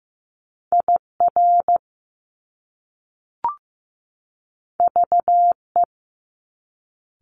Morse code, heard as a keyed tone, about 15 wpm, 700 Hz. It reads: IR VE
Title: Top Two Letter Combinations in Words 15wpm